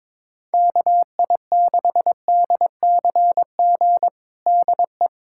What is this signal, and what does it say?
Morse code, heard as a keyed tone, about 22 wpm, 700 Hz. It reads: KI6DCG DE
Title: USA Callsign DE Callsign - 6 Characters - 22wpm